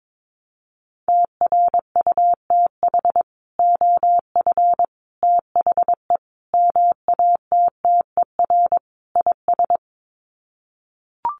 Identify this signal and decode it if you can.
Morse code, heard as a keyed tone, about 22 wpm, 700 Hz. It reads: TRUTH OF THE MATTER IS
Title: Sets of 5 Words Encore2 22wpm